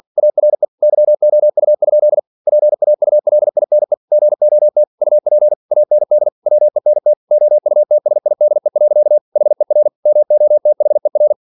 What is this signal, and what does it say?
Morse code, heard as a keyed tone, about 47 wpm, 600 Hz. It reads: ARE YOU? PAULINE GOT UP AND WENT OUTSIDE. HER MOTHER
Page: Never 47wpm